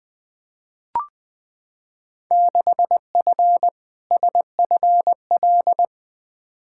Morse code, heard as a keyed tone, about 20 wpm, 700 Hz. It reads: 6F SFL
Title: Field Day Exchanges Encore - 20wpm